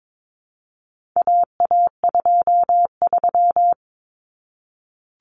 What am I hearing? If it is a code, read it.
Morse code, heard as a keyed tone, about 22 wpm, 700 Hz. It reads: AA23
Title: Common Cut Numbers - 4-Digit 22wpm